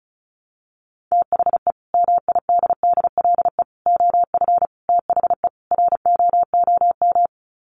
Morse code, heard as a keyed tone, about 34 wpm, 700 Hz. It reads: THE MIDDLE OF THE ROOM
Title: Sets of 5 Words Encore2 35wpm